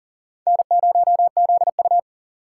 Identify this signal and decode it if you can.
Morse code, heard as a keyed tone, about 52 wpm, 700 Hz. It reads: N0ZU